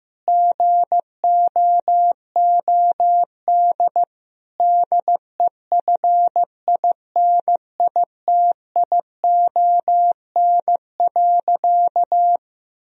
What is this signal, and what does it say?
Morse code, heard as a keyed tone, about 16 wpm, 700 Hz. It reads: GOOD DEFINITION.
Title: Sentences from Top 1000 Words - Review Part 4 - 15wpm